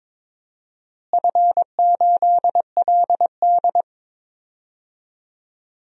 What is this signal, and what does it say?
Morse code, heard as a keyed tone, about 22 wpm, 700 Hz. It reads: F8LD
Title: DXCC-100 Call Signs - 4 Characters - Repeated 3x - 22wpm